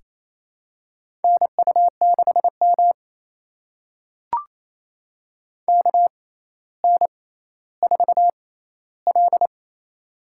Morse code, heard as a keyed tone, about 28 wpm, spaced as for 14 wpm, 700 Hz. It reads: NU6M KN4L
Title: USA Call Signs - 4 Characters - 6x Character Spacing - 28wpm